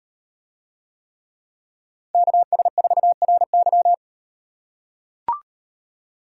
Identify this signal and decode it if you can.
Morse code, heard as a keyed tone, about 36 wpm, 700 Hz. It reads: KS4RY